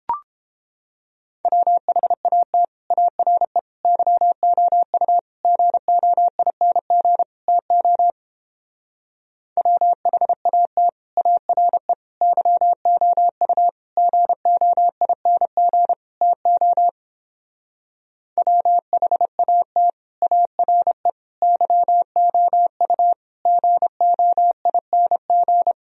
Morse code, heard as a keyed tone, about 29 wpm, 700 Hz. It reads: WHAT ARE YOU GOING TO WHAT ARE YOU GOING TO WHAT ARE YOU GOING